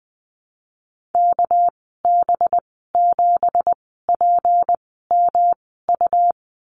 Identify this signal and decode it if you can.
Morse code, heard as a keyed tone, about 19 wpm, spaced as for 14 wpm, 700 Hz. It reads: KB7PMU
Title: 200 Lesson 32g - Letter k - Call Signs 20wpm